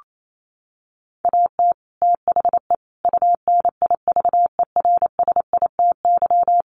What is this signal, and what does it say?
Morse code, heard as a keyed tone, about 28 wpm, 700 Hz. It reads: AT THE UNIVERSITY
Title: Sets of 5 Words Encore2 28wpm